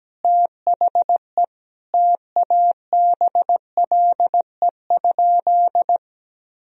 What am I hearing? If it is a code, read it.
Morse code, heard as a keyed tone, about 17 wpm, 700 Hz. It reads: THE TABLE?
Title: Sentences from Top 1000 Words - Review Part 4 - 17wpm